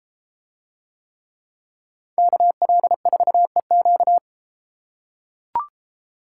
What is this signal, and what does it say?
Morse code, heard as a keyed tone, about 32 wpm, 700 Hz. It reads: KL4EQ